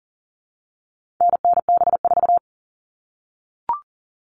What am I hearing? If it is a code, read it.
Morse code, heard as a keyed tone, about 40 wpm, 700 Hz. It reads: NNB4
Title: All Cut Numbers - 4-Digit 40wpm